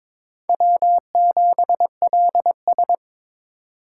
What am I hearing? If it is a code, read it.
Morse code, heard as a keyed tone, about 23 wpm, 700 Hz. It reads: W7LS
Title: USA Call Signs - 4 Characters - 6x Character Spacing - 22wpm